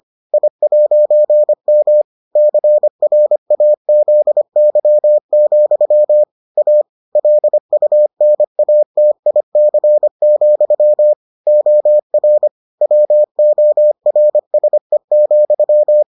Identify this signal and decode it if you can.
Morse code, heard as a keyed tone, about 25 wpm, 600 Hz. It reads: I'M CRAZY, A LUNATIC, OR WORSE,